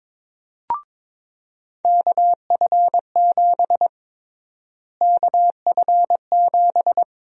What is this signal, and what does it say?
Morse code, heard as a keyed tone, about 23 wpm, 700 Hz. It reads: KF7 KF7
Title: USA Call Sign Prefixes - Repeated 2x - 22wpm